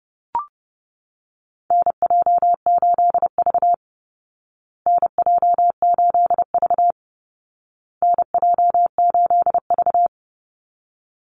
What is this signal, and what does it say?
Morse code, heard as a keyed tone, about 30 wpm, 700 Hz. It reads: NJ8V NJ8V NJ8V
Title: Call Signs Repeated 2x - 30wpm